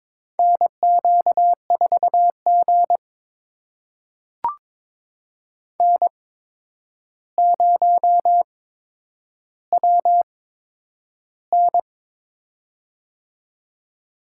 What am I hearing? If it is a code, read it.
Morse code, heard as a keyed tone, about 22 wpm, spaced as for 9 wpm, 700 Hz. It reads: NQ4G N0WN
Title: USA Call Signs - 4 Characters - 8x Character Spacing - 22wpm